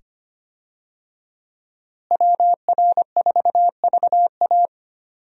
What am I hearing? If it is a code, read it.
Morse code, heard as a keyed tone, about 25 wpm, 700 Hz. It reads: WR4VA
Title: USA Call Signs - 5 Characters - Repeated 2x - 25wpm